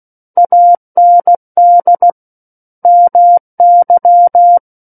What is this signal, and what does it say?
Morse code, heard as a keyed tone, about 16 wpm, 700 Hz. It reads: AND MY
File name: Book One / 07-Chapter 7 - 16 WPM